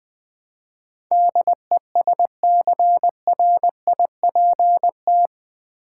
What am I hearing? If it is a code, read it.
Morse code, heard as a keyed tone, about 20 wpm, 700 Hz. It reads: DESCRIPT
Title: Build A Word - 4 Letters At A Time - 11-Letter Words - 20wpm